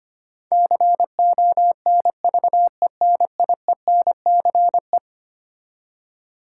Morse code, heard as a keyed tone, about 25 wpm, 700 Hz. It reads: CONVENIENCE
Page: morse-code-ninja-practice-25wpm